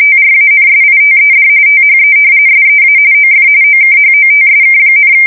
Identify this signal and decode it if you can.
RTTY, about 45.45 baud. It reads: TRANSITED THE SEA OF OKHOTSK,
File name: fsk170 / battleshipnj